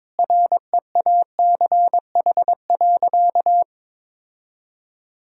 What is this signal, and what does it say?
Morse code, heard as a keyed tone, about 21 wpm, 700 Hz. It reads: REACH.
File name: practice / 22wpm / Sentences with 2-Word Phrases - 22wpm